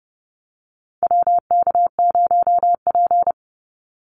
Morse code, heard as a keyed tone, about 30 wpm, 700 Hz. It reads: WK0P
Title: Call Signs Repeated 2x - 30wpm